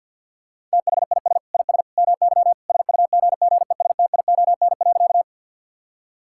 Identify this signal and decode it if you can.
Morse code, heard as a keyed tone, about 50 wpm, 700 Hz. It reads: THIS IS MY SUGGESTION.
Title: Sentences from Top 1600 Words 50wpm